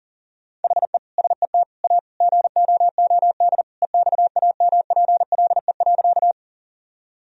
Morse code, heard as a keyed tone, about 37 wpm, 700 Hz. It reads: HE SET A GOOD EXAMPLE.
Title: Sentences from Top 1000 Words - Review Part 1 - 40wpm